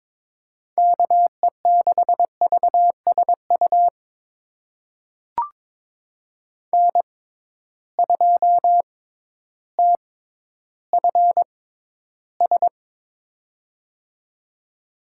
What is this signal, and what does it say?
Morse code, heard as a keyed tone, about 22 wpm, spaced as for 12 wpm, 700 Hz. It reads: KE6VSU N2TFS
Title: USA Call Signs - 6x Character Spacing - 22wpm